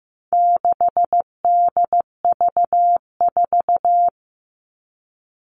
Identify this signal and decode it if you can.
Morse code, heard as a keyed tone, about 15 wpm, 700 Hz. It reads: 6DV4
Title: All Cut Numbers - 4-Digit 15wpm